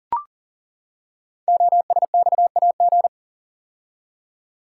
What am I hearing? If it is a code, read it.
Morse code, heard as a keyed tone, about 39 wpm, 690 Hz. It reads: OSXAG
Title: Code Groups - 5 Letters - 40wpm